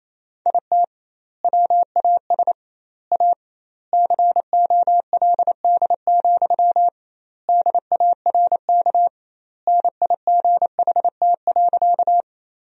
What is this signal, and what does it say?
Morse code, heard as a keyed tone, about 28 wpm, 700 Hz. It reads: IT WAS A COLD, DARK NIGHT.
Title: 2x Word-Spacing - Sentences from Top 500 Words - 28wpm